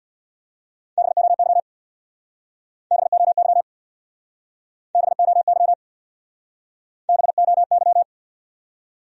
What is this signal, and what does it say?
Morse code, heard as a keyed tone, about 65 wpm, 700 Hz. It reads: BOY BOY BOY BOY